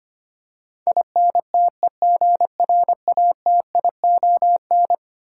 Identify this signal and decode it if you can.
Morse code, heard as a keyed tone, about 25 wpm, 700 Hz. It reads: INTEGRATION
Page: morse-code-ninja-practice-25wpm